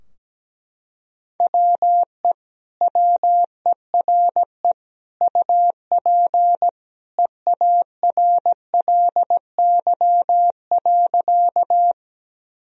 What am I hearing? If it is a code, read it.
Morse code, heard as a keyed tone, about 17 wpm, 700 Hz. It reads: WE WERE UP EARLY.
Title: Sentences from Top 1000 Words - Review Part 2 - 17wpm